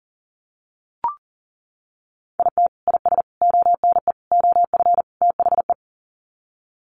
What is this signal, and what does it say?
Morse code, heard as a keyed tone, about 40 wpm, 700 Hz. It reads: IT IS ONE OF THE
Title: Sets of 5 Words 40wpm